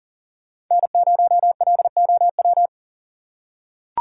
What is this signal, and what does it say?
Morse code, heard as a keyed tone, about 40 wpm, 700 Hz. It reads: N0LOW
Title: USA Call Signs - 5 Characters - 6x Character Spacing - 40wpm